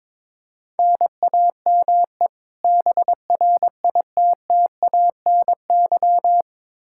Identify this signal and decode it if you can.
Morse code, heard as a keyed tone, about 21 wpm, 700 Hz. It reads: NAME BRITTANY
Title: K1USN SST QSO Practice 22wpm